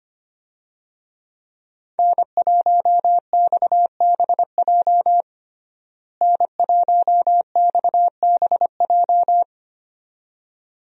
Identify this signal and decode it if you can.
Morse code, heard as a keyed tone, about 25 wpm, 700 Hz. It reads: N1XBJ N1XBJ
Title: USA Call Signs - 5 Characters - Repeated 2x - 25wpm